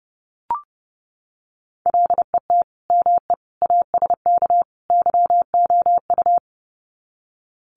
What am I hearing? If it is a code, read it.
Morse code, heard as a keyed tone, about 30 wpm, 700 Hz. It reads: LET ME ASK YOU
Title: Sets of 4 Words 30wpm